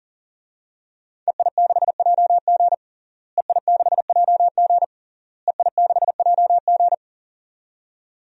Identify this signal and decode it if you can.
Morse code, heard as a keyed tone, about 40 wpm, 700 Hz. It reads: EI6JG EI6JG EI6JG
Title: DXCC-100 Call Signs - 5 Characters - Repeated 3x - 40wpm